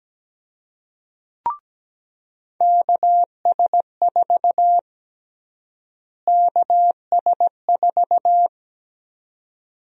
Morse code, heard as a keyed tone, about 17 wpm, 700 Hz. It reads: KS4 KS4
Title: USA Call Sign Prefixes - Repeated 2x - 17wpm